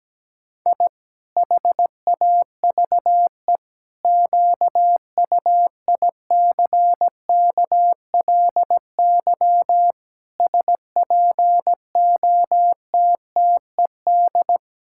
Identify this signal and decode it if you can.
Morse code, heard as a keyed tone, about 17 wpm, 700 Hz. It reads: I HAVE QUICKLY SPOTTED
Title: Pangram Challenge v2 - 17wpm